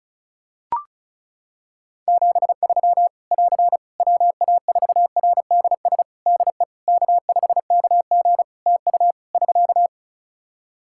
Morse code, heard as a keyed tone, about 34 wpm, 700 Hz. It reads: 73 <AR> WA4RDS DE K5KG TU <SK>
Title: Practice QSO - Exchange 4 - Ending - 35wpm